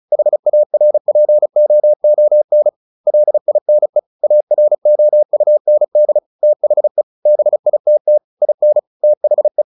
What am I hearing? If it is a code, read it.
Morse code, heard as a keyed tone, about 34 wpm, 600 Hz. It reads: HARPOON LINE AROUND THE BITT IN THE
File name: the old man and the sea 2 35wpm